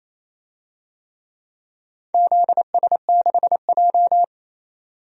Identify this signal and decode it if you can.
Morse code, heard as a keyed tone, about 28 wpm, 700 Hz. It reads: ZS6J
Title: DXCC-100 Call Signs - 4 Characters - 28wpm